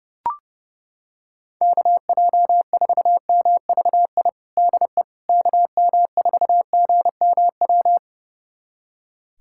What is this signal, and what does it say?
Morse code, heard as a keyed tone, about 30 wpm, 700 Hz. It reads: KJ4MVI DE KM4GMW
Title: USA Callsign DE Callsign - 6 Characters - 30wpm